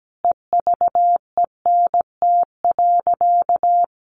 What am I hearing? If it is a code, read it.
Morse code, heard as a keyed tone, about 17 wpm, 700 Hz. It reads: EVENT.
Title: Sentences from Top 1400 Words 17wpm